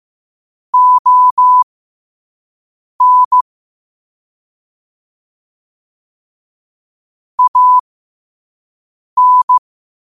Morse code, heard as a keyed tone, about 15 wpm, spaced as for 5 wpm, 1000 Hz. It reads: ON AN